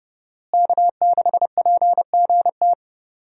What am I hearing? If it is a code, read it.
Morse code, heard as a keyed tone, about 30 wpm, 700 Hz. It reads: K6PGT
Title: USA Call Signs - 5 Characters - Repeated 3x - 30wpm